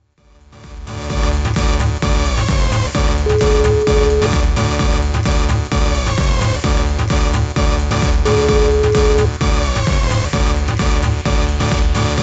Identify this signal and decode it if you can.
Ringback tone (Europe, 425 Hz), with noise.